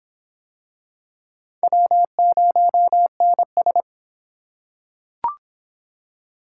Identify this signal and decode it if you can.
Morse code, heard as a keyed tone, about 24 wpm, 700 Hz. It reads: W0NS